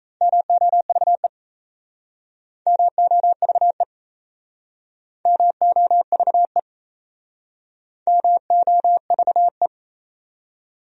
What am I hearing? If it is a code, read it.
Morse code, heard as a keyed tone, about 34 wpm, 700 Hz. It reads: MOVE MOVE MOVE MOVE
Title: Speed-Racing Top 300 Words 28wpm